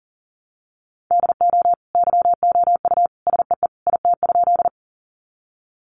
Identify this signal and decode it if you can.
Morse code, heard as a keyed tone, about 40 wpm, 700 Hz. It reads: DO YOU SEE IT?
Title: Sentences from Top 100 Words 40wpm